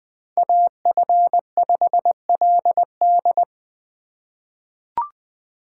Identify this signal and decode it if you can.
Morse code, heard as a keyed tone, about 20 wpm, 700 Hz. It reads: AF5LD
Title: USA Call Signs - 5 Characters - 4x Character Spacing - 20wpm